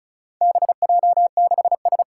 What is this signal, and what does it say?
Morse code, heard as a keyed tone, about 36 wpm, 700 Hz. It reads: BJ6S
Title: Four Challenging Letter-Number Combinations 35wpm